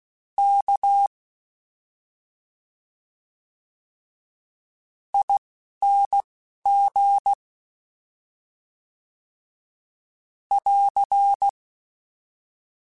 Morse code, heard as a keyed tone, about 16 wpm, 800 Hz. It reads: K I N G <AR>
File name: Special INGs-mp3s / Special INGs-mp3s / Slow ing @10